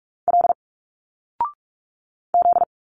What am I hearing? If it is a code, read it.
Morse code, heard as a keyed tone, about 46 wpm, 700 Hz. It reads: L Z
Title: Single Letters 45wpm